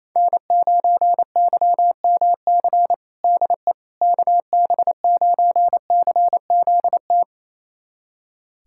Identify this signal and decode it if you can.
Morse code, heard as a keyed tone, about 28 wpm, 700 Hz. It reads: N9YMC DE KB9CZT
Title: USA Callsign DE Callsign 28wpm